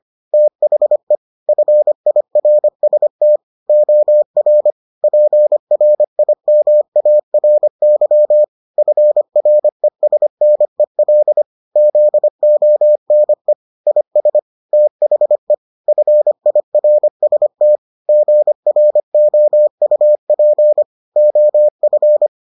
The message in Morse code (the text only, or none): THE FIRST OR PRIMARY FRESNEL ZONE IS THE FIRST GROUP OF